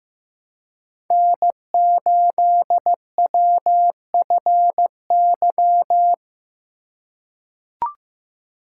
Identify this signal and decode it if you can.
Morse code, heard as a keyed tone, about 15 wpm, 700 Hz. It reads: N8WFY
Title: USA Call Signs - 5 Characters - 2x Character Spacing - 15wpm